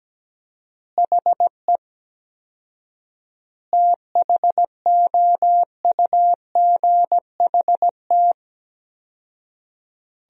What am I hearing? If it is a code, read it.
Morse code, heard as a keyed tone, about 17 wpm, 700 Hz. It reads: HE THOUGHT
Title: 4x Word-Spacing - Sentences from Top 300 Words - 17wpm